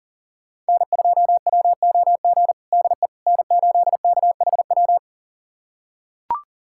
Morse code, heard as a keyed tone, about 40 wpm, 700 Hz. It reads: N2WOG DE N8KHW
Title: USA Callsign DE Callsign - 5 Characters - 40wpm